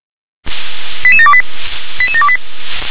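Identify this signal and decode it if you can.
Five-tone selective call (ZVEI), heard as quick runs of five tones, tones of about 70 ms. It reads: 88728 88728
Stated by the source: ZVEI five-tone selective call received on SDR.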